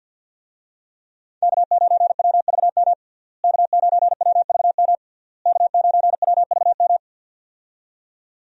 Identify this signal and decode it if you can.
Morse code, heard as a keyed tone, about 49 wpm, 700 Hz. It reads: K9WVM K9WVM K9WVM
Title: USA Call Signs - 5 Characters - Repeated 3x - 50wpm